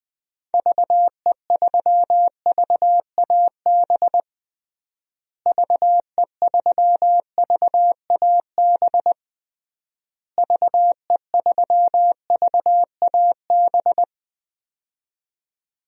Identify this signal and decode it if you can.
Morse code, heard as a keyed tone, about 20 wpm, 700 Hz. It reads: VE3VAB VE3VAB VE3VAB
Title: DXCC-100 Call Signs - 6 Characters - Repeated 3x - 20wpm